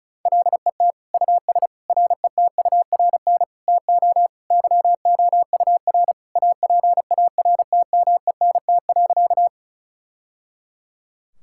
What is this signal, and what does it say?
Morse code, heard as a keyed tone, about 36 wpm, 690 Hz. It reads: LET US RETURN TO YOUR APARTMENT.
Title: Sentences from Top 1000 Words - Review Part 6 - 35wpm